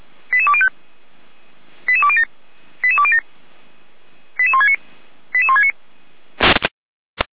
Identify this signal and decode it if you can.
Five-tone selective call (ZVEI), heard as quick runs of five tones, tones of about 70 ms. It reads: 88205 88207 88207 88169 88169